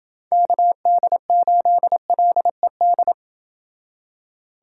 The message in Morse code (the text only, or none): KD8LED